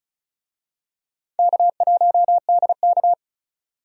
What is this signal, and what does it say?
Morse code, heard as a keyed tone, about 36 wpm, 700 Hz. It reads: K1DK
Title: USA Call Signs - 4 Characters - 35wpm